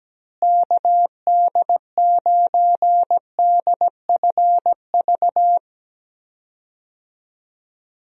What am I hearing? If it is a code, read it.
Morse code, heard as a keyed tone, about 17 wpm, 700 Hz. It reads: KD9DFV
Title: USA Call Signs - 6 Characters - Repeated 3x - 17wpm